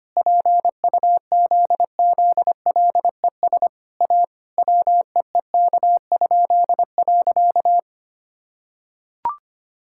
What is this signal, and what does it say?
Morse code, heard as a keyed tone, about 25 wpm, 700 Hz. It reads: PUZZLES A WEEK?.
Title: Pangram Challenge v2 - 25wpm